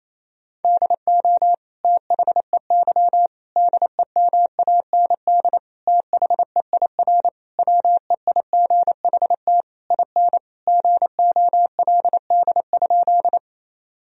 Morse code, heard as a keyed tone, about 28 wpm, 700 Hz. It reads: DO THEY DEMAND THEIR WEIGHT IN GOLD?
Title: Sentences from Top 1000 Words - Review Part 3 - 28wpm